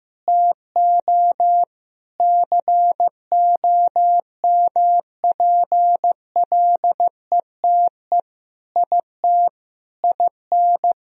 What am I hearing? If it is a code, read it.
Morse code, heard as a keyed tone, about 16 wpm, 700 Hz. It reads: TO COMPLETE IT IN
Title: Sentences from Top 1000 Words - Review Part 3 - 15wpm